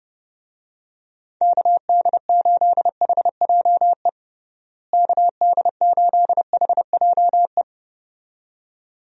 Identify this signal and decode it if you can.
Morse code, heard as a keyed tone, about 30 wpm, 700 Hz. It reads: KD8HJE KD8HJE
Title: USA Call Signs - 6 Characters - Repeated 2x - 30wpm